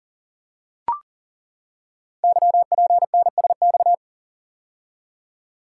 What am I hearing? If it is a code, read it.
Morse code, heard as a keyed tone, about 39 wpm, 700 Hz. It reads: YPNSX
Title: Code Groups - 5 Letter:Numbers - 40wpm